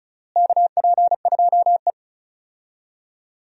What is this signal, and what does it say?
Morse code, heard as a keyed tone, about 34 wpm, 700 Hz. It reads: KP2E
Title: DXCC-100 Call Signs - 4 Characters - Repeated 3x - 35wpm